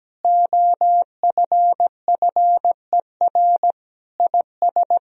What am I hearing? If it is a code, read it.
Morse code, heard as a keyed tone, about 17 wpm, 700 Hz. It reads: OFFER IS
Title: Sentences from Top 1000 Words - Review Part 4 - 17wpm